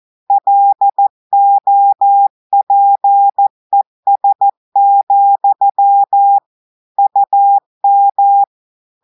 Morse code, heard as a keyed tone, about 14 wpm, 800 Hz. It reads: LOPES, UM
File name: audios / portugais / portugais texte en clair 1 vitesse 14 mots minutes